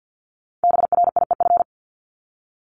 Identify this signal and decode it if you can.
Morse code, heard as a keyed tone, about 50 wpm, 700 Hz. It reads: BRIEF
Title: Multi-letter Phonemes Repeated 3x 50wpm